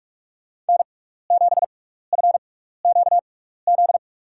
Morse code, heard as a keyed tone, about 46 wpm, spaced as for 16 wpm, 700 Hz. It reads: N7FQZ
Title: USA Call Signs - 5 Characters - 6x Character Spacing - 45wpm